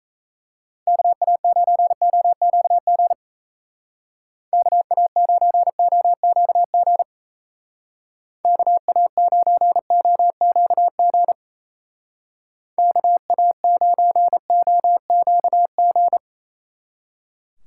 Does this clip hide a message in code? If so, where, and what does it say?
Yes: Morse code from 0.9 to 16.2 s, heard as a keyed tone, about 35 wpm, 700 Hz. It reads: KA9OQG KA9OQG KA9OQG KA9OQG